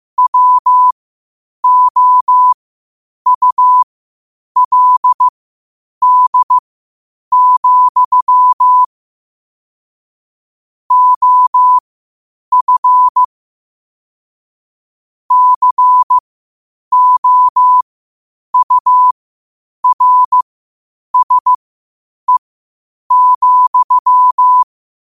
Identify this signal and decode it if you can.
Morse code, heard as a keyed tone, about 15 wpm, spaced as for 8 wpm, 1000 Hz. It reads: WOULD, OF COURSE,
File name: qotdfiles / 8wpm